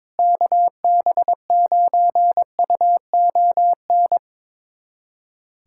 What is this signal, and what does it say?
Morse code, heard as a keyed tone, about 22 wpm, 700 Hz. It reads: KB9UON
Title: USA Callsign DE Callsign - 6 Characters - 22wpm